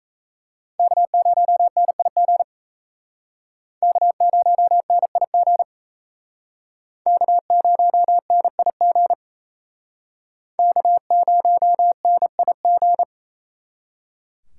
Morse code, heard as a keyed tone, about 37 wpm, 700 Hz. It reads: K0NIG K0NIG K0NIG K0NIG